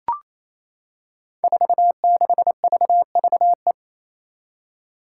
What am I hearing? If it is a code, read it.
Morse code, heard as a keyed tone, about 27 wpm, 700 Hz. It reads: 46VVE
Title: Five Challenging Letter-Number Combinations 28wpm